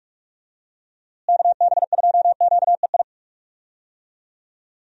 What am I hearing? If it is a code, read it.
Morse code, heard as a keyed tone, about 43 wpm, 700 Hz. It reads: KB2QEI
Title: USA Call Signs - 6 Characters - Repeated 2x - 45wpm